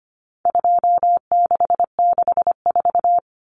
Morse code, heard as a keyed tone, about 25 wpm, 700 Hz. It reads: 2664
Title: Common Cut Numbers - 4-Digit 25wpm